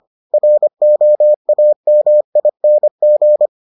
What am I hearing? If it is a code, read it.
Morse code, heard as a keyed tone, about 25 wpm, 600 Hz. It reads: ROAMING